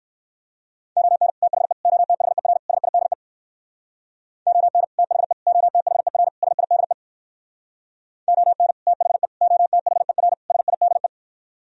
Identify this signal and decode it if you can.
Morse code, heard as a keyed tone, about 67 wpm, 700 Hz. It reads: ON THE OTHER SIDE ON THE OTHER SIDE ON THE OTHER SIDE